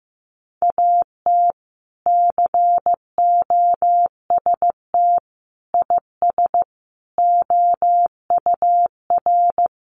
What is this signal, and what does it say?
Morse code, heard as a keyed tone, about 15 wpm, 700 Hz. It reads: AT COST IS OUR
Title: Sentences from Top 1500 Words 15wpm